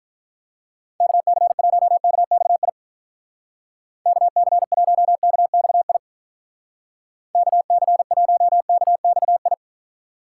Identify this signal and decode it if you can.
Morse code, heard as a keyed tone, about 44 wpm, 700 Hz. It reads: KC1KXI KC1KXI KC1KXI